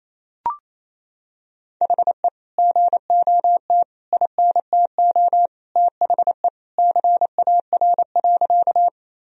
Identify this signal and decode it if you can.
Morse code, heard as a keyed tone, about 28 wpm, 700 Hz. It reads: HE GOT INTO THE CAR.
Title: Sentences from Top 1000 Words - Review Part 1 - 28wpm